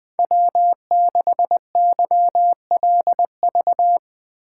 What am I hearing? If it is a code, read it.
Morse code, heard as a keyed tone, about 21 wpm, 700 Hz. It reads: W6YLV
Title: USA Call Signs - 5 Characters - 4x Character Spacing - 20wpm